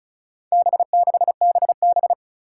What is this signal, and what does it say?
Morse code, heard as a keyed tone, about 36 wpm, 700 Hz. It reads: B6BB
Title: Four Challenging Letter-Number Combinations 35wpm